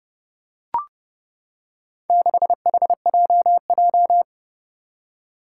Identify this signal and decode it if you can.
Morse code, heard as a keyed tone, about 30 wpm, 700 Hz. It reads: 6HJJ
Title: Four Challenging Letter-Number Combinations 30wpm